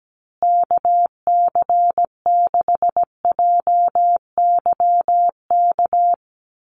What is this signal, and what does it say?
Morse code, heard as a keyed tone, about 17 wpm, 700 Hz. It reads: KC6JYK
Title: USA Call Signs Repeated 3x 17wpm